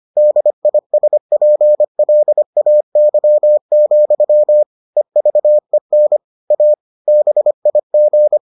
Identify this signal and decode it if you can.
Morse code, heard as a keyed tone, about 25 wpm, 600 Hz. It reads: DISPLAY, EVEN A BIG